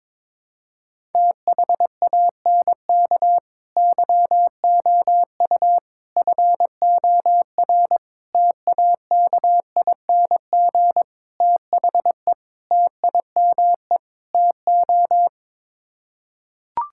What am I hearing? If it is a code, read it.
Morse code, heard as a keyed tone, about 22 wpm, 700 Hz. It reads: THANK YOU FOR TAKING THE TIME TO
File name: practice / 22wpm / Sets of 7 Words - 22wpm